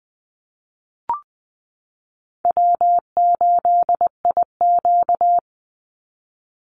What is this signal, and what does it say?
Morse code, heard as a keyed tone, about 20 wpm, 700 Hz. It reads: W8IQ
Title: Call Signs Repeated 2x - 20wpm